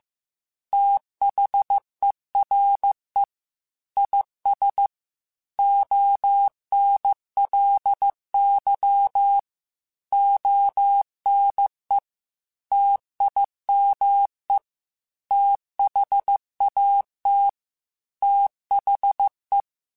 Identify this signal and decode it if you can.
Morse code, heard as a keyed tone, about 15 wpm, 800 Hz. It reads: THERE IS ONLY ONE TIME THAT THE
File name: Technical Subjects / Technical Subjects-mp3s / OR Gate @13